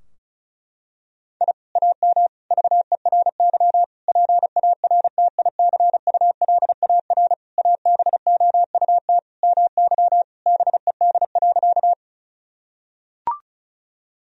Morse code, heard as a keyed tone, about 34 wpm, 700 Hz. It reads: I AM VERY PARTICULAR ABOUT MY BED.
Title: Sentences from Top 1000 Words - Review Part 4 - 35wpm